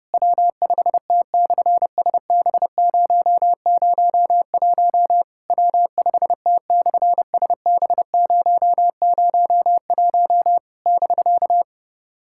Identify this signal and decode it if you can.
Morse code, heard as a keyed tone, about 30 wpm, 700 Hz. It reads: W5T/SB001 W5T/SB001 <BK>
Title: SOTA QSO Practice 30wpm